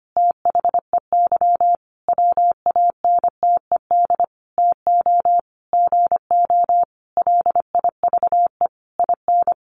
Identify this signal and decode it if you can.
Morse code, heard as a keyed tone, about 25 wpm, 700 Hz. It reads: THEY WANTED TO GO LIVE IN
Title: A Sound Of Thunder 25wpm